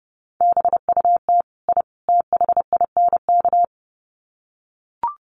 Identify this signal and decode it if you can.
Morse code, heard as a keyed tone, about 30 wpm, 700 Hz. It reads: BUT I THINK
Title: Sets of 3 Words 30wpm